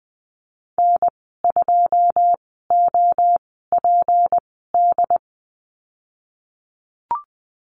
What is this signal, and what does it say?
Morse code, heard as a keyed tone, about 20 wpm, spaced as for 14 wpm, 700 Hz. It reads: N2OPD
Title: 154 Lesson 25g - Letter p - Call Signs 20wpm